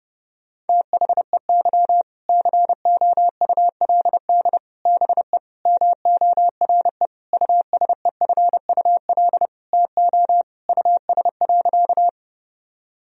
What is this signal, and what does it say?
Morse code, heard as a keyed tone, about 30 wpm, 700 Hz. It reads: THEY COULD BE MORE USEFUL TO US.
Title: Sentences from Top 1000 Words - Review Part 4 - 30wpm